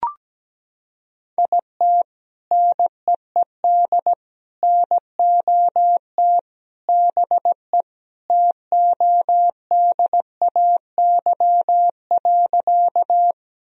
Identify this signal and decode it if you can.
Morse code, heard as a keyed tone, about 17 wpm, 700 Hz. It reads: IT NEED NOT BE TODAY.
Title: Sentences from Top 1000 Words - Review Part 4 - 17wpm